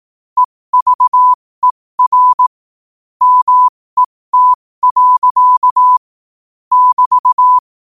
Morse code, heard as a keyed tone, about 18 wpm, spaced as for 15 wpm, 1000 Hz. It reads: EVER MET. <BT>
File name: qotdfiles / 15wpm